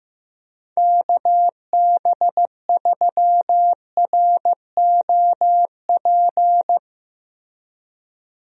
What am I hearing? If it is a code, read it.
Morse code, heard as a keyed tone, about 15 wpm, 700 Hz. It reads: KB3ROP
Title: USA Call Signs - 6 Characters - Repeated 3x - 15wpm